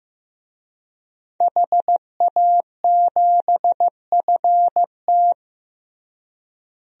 Morse code, heard as a keyed tone, about 16 wpm, 700 Hz. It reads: HA7FT
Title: DXCC-100 Call Signs - 5 Characters - Repeated 3x - 15wpm